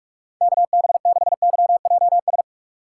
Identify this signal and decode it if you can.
Morse code, heard as a keyed tone, about 46 wpm, 700 Hz. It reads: KB6YJS
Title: USA Call Signs - 6 Characters - 6x Character Spacing - 45wpm